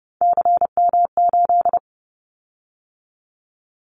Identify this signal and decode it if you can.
Morse code, heard as a keyed tone, about 30 wpm, 700 Hz. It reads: CM8
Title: Callsign Prefixes for Top 100 DXCC - 30wpm